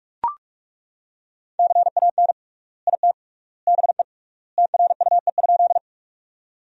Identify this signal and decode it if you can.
Morse code, heard as a keyed tone, about 42 wpm, 700 Hz. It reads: CAN IT BE TRUE?
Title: 3x Word-Spacing - Sentences from Top 400 Words - 45wpm